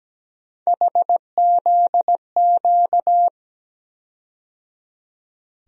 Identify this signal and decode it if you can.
Morse code, heard as a keyed tone, about 17 wpm, 700 Hz. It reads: HZQ
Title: USA Call Sign Postfixes - 3 Letters 17wpm